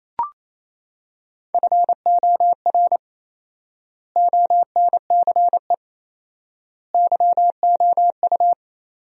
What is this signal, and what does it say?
Morse code, heard as a keyed tone, about 28 wpm, 700 Hz. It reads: FOR ONCE YOU
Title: 4x Word-Spacing - Sentences from Top 300 Words - 28wpm